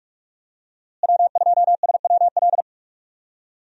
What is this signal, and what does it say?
Morse code, heard as a keyed tone, about 45 wpm, 700 Hz. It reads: W2SWL
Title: USA Call Signs - 5 Characters - Repeated 2x - 45wpm